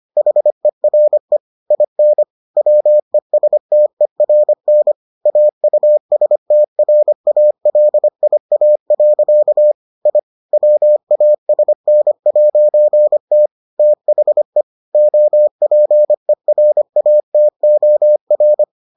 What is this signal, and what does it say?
Morse code, heard as a keyed tone, about 25 wpm, 600 Hz. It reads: HERE IN WESTERN AUSTRALIA. I WASN'T THE OPERATOR